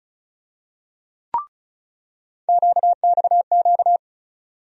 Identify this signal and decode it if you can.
Morse code, heard as a keyed tone, about 36 wpm, 700 Hz. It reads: QXQ
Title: USA Call Sign Postfixes - 3 Letters 35wpm